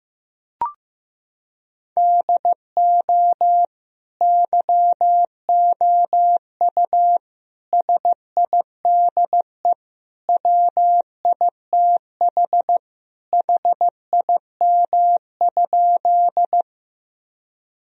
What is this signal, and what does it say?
Morse code, heard as a keyed tone, about 16 wpm, 700 Hz. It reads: DO YOU SIDE WITH HIM?
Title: Sentences from Top 1000 Words - Review Part 1 - 15wpm